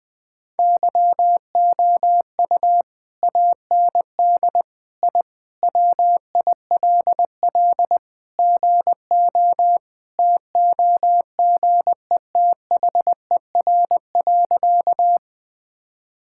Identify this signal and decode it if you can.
Morse code, heard as a keyed tone, about 21 wpm, 700 Hz. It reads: YOU AND I WILL GO TOGETHER.
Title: Sentences with 2-Word Phrases - Spoken Beginning - 20wpm